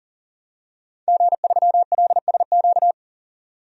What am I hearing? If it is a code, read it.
Morse code, heard as a keyed tone, about 40 wpm, 700 Hz. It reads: G3LSQ
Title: DXCC-100 Call Signs - 5 Characters - 40wpm